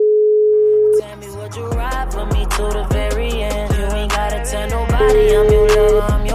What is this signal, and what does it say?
Ringback tone (Europe, 425 Hz), with noise.